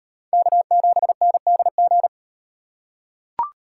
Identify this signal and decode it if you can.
Morse code, heard as a keyed tone, about 38 wpm, 700 Hz. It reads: K7NDG